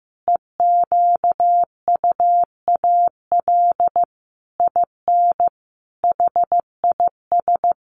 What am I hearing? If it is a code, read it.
Morse code, heard as a keyed tone, about 15 wpm, 700 Hz. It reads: EQUAL IN HIS
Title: Sentences from Top 1400 Words 15wpm